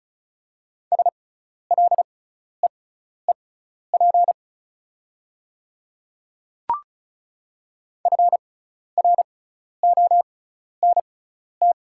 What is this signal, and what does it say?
Morse code, heard as a keyed tone, about 34 wpm, spaced as for 12 wpm, 690 Hz. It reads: SLEEP FRONT
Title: 6x Character Spacing - Top 500 Words - 35wpm